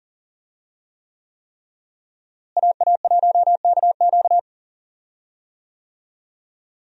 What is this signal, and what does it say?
Morse code, heard as a keyed tone, about 40 wpm, 700 Hz. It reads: AA1KQ
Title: USA Call Signs - 5 Characters - Repeated 3x - 40wpm